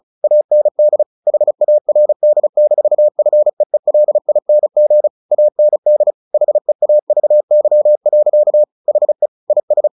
Morse code, heard as a keyed tone, about 34 wpm, 600 Hz. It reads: AND HARD-FEELING AND HEAVY. HE IS
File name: the old man and the sea 2 35wpm